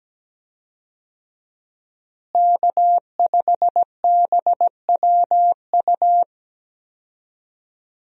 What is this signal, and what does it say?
Morse code, heard as a keyed tone, about 17 wpm, 700 Hz. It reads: K5BWU
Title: USA Call Signs - 5 Characters - Repeated 2x - 17wpm